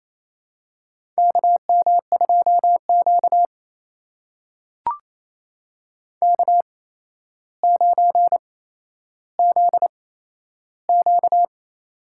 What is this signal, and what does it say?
Morse code, heard as a keyed tone, about 28 wpm, spaced as for 12 wpm, 700 Hz. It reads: KM2Q K9ZQ
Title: USA Call Signs - 4 Characters - 8x Character Spacing - 28wpm